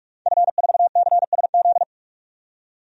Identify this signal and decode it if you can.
Morse code, heard as a keyed tone, about 46 wpm, 700 Hz. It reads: F4CSZ
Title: DXCC-100 Call Signs - 5 Characters - 45wpm